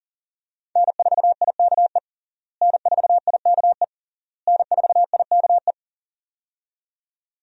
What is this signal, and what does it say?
Morse code, heard as a keyed tone, about 37 wpm, 700 Hz. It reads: N4IKE N4IKE N4IKE
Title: USA Call Signs - 5 Characters - Repeated 3x - 40wpm